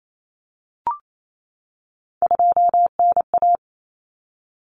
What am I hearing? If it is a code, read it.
Morse code, heard as a keyed tone, about 28 wpm, 700 Hz. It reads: 2NA
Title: Common Cut Numbers - 3-Digit 28wpm